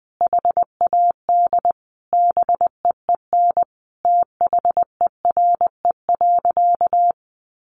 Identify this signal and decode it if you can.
Morse code, heard as a keyed tone, about 20 wpm, 700 Hz. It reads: HAD BEEN THERE.
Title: Sentences from Top 300 Words 20wpm